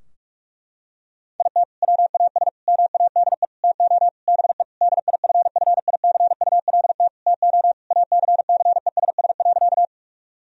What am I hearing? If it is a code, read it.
Morse code, heard as a keyed tone, about 46 wpm, 700 Hz. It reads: IT WAS MADE TO BE DIFFICULT TO ACCESS.
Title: Sentences from Top 1000 Words - Review Part 4 - 45wpm